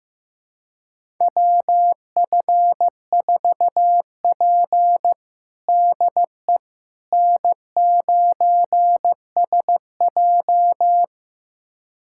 Morse code, heard as a keyed tone, about 15 wpm, 700 Hz. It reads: WF4P DE N9SJ
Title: USA Callsign DE Callsign - 4 Characters - 15wpm